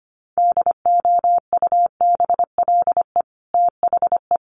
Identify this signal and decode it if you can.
Morse code, heard as a keyed tone, about 25 wpm, 700 Hz. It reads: DOUBLE THE
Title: Sentences from Top 1400 Words 25wpm